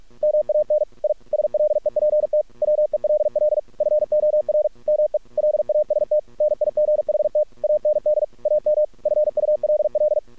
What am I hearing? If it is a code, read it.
Morse code, heard as a keyed tone, about 46 wpm, 600 Hz. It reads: MAN I SLEPT WELL FOR ME LAST NIGHT AND AM FULL
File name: archived / CRSnet